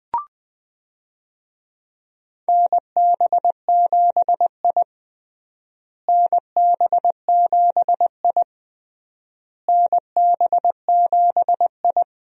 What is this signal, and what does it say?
Morse code, heard as a keyed tone, about 20 wpm, 700 Hz. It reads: NB7I NB7I NB7I
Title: USA Call Signs - 4 Characters - Repeated 3x - 20wpm